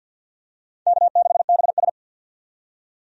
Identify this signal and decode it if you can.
Morse code, heard as a keyed tone, about 47 wpm, 700 Hz. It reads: K6BS
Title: USA Call Signs - 4 Characters - 50wpm